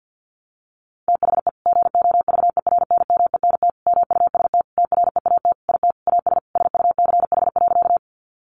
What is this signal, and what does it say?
Morse code, heard as a keyed tone, about 50 wpm, 700 Hz. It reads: THE GOVERNMENT MUST TREAT IT AS SUCH.
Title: Sentences from Top 1300 Words 50wpm